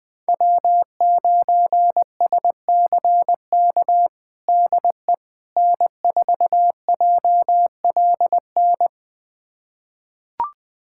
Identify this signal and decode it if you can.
Morse code, heard as a keyed tone, about 19 wpm, 700 Hz. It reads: W9SCK DE N4JLN
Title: USA Callsign DE Callsign - 5 Characters - 20wpm